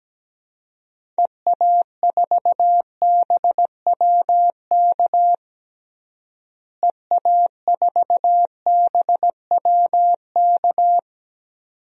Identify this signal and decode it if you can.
Morse code, heard as a keyed tone, about 17 wpm, 700 Hz. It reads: EA4BWK EA4BWK
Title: DXCC-100 Call Signs - 6 Characters - Repeated 2x - 17wpm